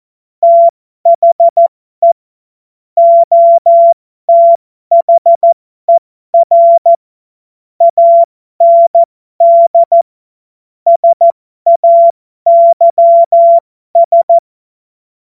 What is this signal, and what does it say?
Morse code, heard as a keyed tone, about 14 wpm, spaced as for 12 wpm, 680 Hz. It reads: THE OTHER AND SAYS
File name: Morsekurs / Witze